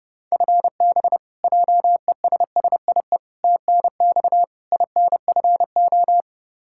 Morse code, heard as a keyed tone, about 30 wpm, 700 Hz. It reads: FB JESSIE TNX INFO
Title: Practice QSO - Exchange 3 - More Info - 30wpm